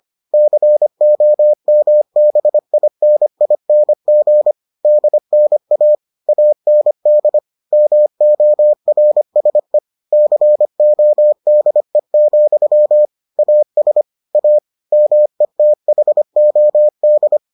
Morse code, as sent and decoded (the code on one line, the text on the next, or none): -.-. --- -- -... .. -. .. -. --. / -.. -. .- / .- -. -.. / -- --- .-. ... / -.-. --- -.. --..-- / .- ... / .- / -- . - .... --- -..
COMBINING DNA AND MORSE CODE, AS A METHOD